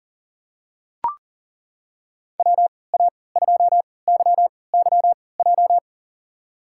Morse code, heard as a keyed tone, about 40 wpm, spaced as for 23 wpm, 700 Hz. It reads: WA2YYJ
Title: USA Call Signs - 6 Characters - 3x Character Spacing - 40wpm